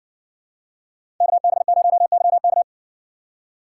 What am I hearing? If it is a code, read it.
Morse code, heard as a keyed tone, about 56 wpm, 700 Hz. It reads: KB0YK